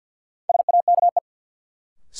Morse code, heard as a keyed tone, about 47 wpm, 700 Hz. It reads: SAKE
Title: Sets of 4 Words - New Word Drill - 50wpm